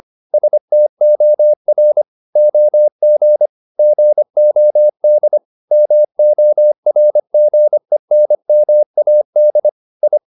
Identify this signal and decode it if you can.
Morse code, heard as a keyed tone, about 25 wpm, 600 Hz. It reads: STOR OG GOD MORGENMAD I
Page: Ferie 25 wpm